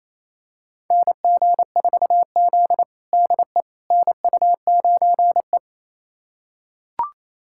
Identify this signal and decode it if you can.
Morse code, heard as a keyed tone, about 28 wpm, 700 Hz. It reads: NG4Z DE NU9E
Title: USA Callsign DE Callsign - 4 Characters - 28wpm